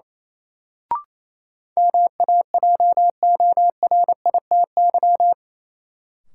Morse code, heard as a keyed tone, about 28 wpm, 700 Hz. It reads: MAJORITY
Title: Sets of 3 Words - New Word Drill - 28wpm